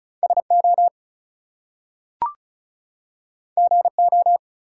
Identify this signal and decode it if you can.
Morse code, heard as a keyed tone, about 36 wpm, 700 Hz. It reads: SO GO
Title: Two Letter Words - 35wpm